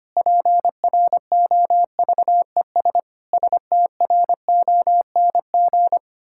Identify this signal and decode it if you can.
Morse code, heard as a keyed tone, about 25 wpm, 700 Hz. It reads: PROVES STRONG
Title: Pangram Challenge v2 - 25wpm